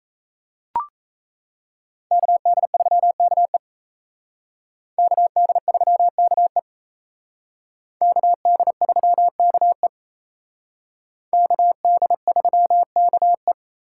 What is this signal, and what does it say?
Morse code, heard as a keyed tone, about 37 wpm, 700 Hz. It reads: KD3KE KD3KE KD3KE KD3KE